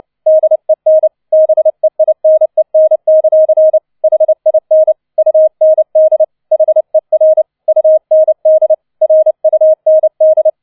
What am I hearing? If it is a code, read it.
Morse code, heard as a keyed tone, about 29 wpm, 630 Hz. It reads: DEN BEINEN; HIN UND HER UND RUND